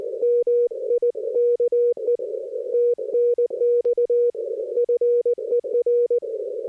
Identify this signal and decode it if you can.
Morse code, heard as a keyed tone, about 20 wpm, 480 Hz. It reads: MIKE TNX FER
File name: 60m qso